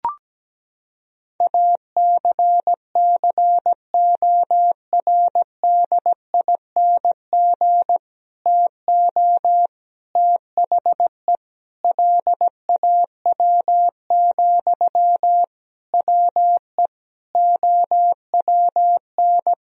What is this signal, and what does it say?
Morse code, heard as a keyed tone, about 17 wpm, 700 Hz. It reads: ACCORDING TO THE LAW, WE OWN
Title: Sentences from Top 1000 Words - Review Part 4 - 17wpm